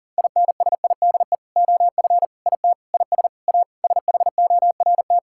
Morse code, heard as a keyed tone, about 40 wpm, 700 Hz. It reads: INSIDE OF IT IS A SHORT
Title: Sentences from Top 1000 Words - Review Part 6 - 40wpm